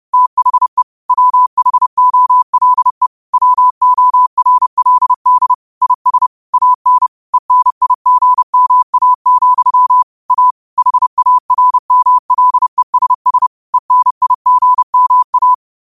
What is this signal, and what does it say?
Morse code, heard as a keyed tone, about 30 wpm, 1000 Hz. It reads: THE WHOLE WORLD IS AN ENIGMA, A HARMLESS ENIGMA
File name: qotdfiles / 30wpm